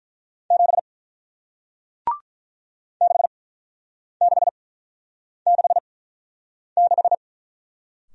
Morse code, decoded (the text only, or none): 7 6666